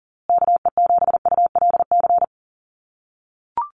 Morse code, heard as a keyed tone, about 40 wpm, 700 Hz. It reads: KE7ULC
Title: USA Call Signs Repeated 2x 40wpm